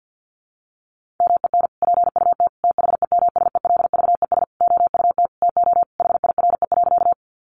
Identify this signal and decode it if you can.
Morse code, heard as a keyed tone, about 48 wpm, 700 Hz. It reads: MEN PUT THEMSELVES OUT TO HIRE.
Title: Sentences from Top 1400 Words 50wpm